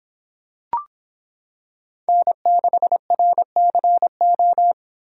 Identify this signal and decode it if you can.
Morse code, heard as a keyed tone, about 26 wpm, 700 Hz. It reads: N6RCO